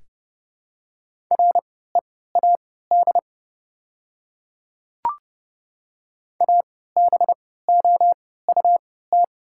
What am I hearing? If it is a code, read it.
Morse code, heard as a keyed tone, about 30 wpm, spaced as for 17 wpm, 690 Hz. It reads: READ ABOUT
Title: 3x Character Spacing - Top 200 Words - 30wpm